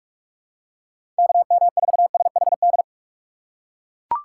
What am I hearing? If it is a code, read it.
Morse code, heard as a keyed tone, about 42 wpm, 700 Hz. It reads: KM4SHD
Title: USA Call Signs - 6 Characters - 8x Character Spacing - 45wpm